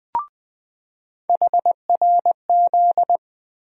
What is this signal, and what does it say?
Morse code, heard as a keyed tone, about 20 wpm, 700 Hz. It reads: HRZ
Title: USA Call Sign Postfixes - 3 Letters 20wpm